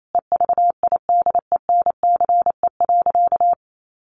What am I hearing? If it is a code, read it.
Morse code, heard as a keyed tone, about 28 wpm, 700 Hz. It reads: EVIDENCE.
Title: Sentences from Top 1300 Words 28wpm